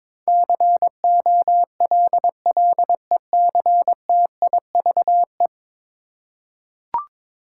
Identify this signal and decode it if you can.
Morse code, heard as a keyed tone, about 22 wpm, 700 Hz. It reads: COLLECTIVE
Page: morse-code-ninja-practice-22wpm